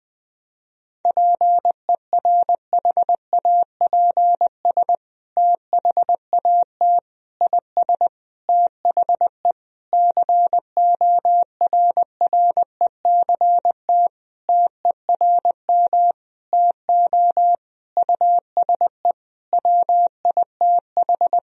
Morse code, heard as a keyed tone, about 20 wpm, 700 Hz. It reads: PERHAPS THAT IS THE CORRECT TERM TO USE WITH
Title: Sentences from Top 1000 Words - Review Part 4 - 20wpm